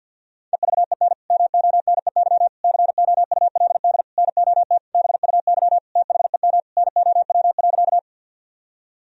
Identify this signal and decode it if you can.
Morse code, heard as a keyed tone, about 50 wpm, 700 Hz. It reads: EVEN MONEY COULD NOT BUY THEM NOW.
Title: Sentences from Top 1000 Words - Review Part 4 - 50wpm